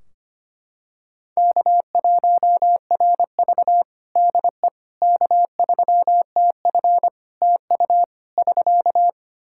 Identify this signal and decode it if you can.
Morse code, heard as a keyed tone, about 25 wpm, 700 Hz. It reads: K1RV DE K3TF TU <SK>
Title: Practice QSO - Exchange 4 - Ending - 25wpm